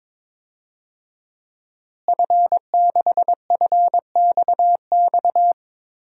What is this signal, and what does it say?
Morse code, heard as a keyed tone, about 22 wpm, 700 Hz. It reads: F6FXX
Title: DXCC-100 Call Signs - 5 Characters - Repeated 3x - 22wpm